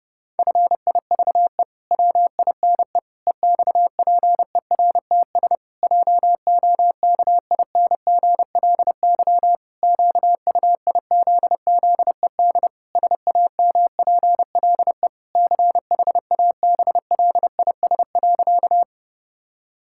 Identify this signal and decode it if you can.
Morse code, heard as a keyed tone, about 31 wpm, 700 Hz. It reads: FIVE WINE EXPERTS JOKINGLY QUIZZED SAMPLE CHABLIS.
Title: Pangram Challenge v2 - 30wpm